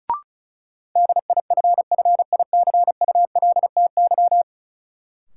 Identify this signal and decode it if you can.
Morse code, heard as a keyed tone, about 36 wpm, 700 Hz. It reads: DIFFICULTY
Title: Sentences from Top 1500 Words - New Word Drill Alt - 35wpm